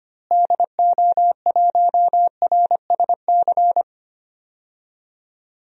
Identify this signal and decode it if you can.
Morse code, heard as a keyed tone, about 25 wpm, 700 Hz. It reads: DO1RSC
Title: DXCC-100 Call Signs - 6 Characters - Repeated 2x - 25wpm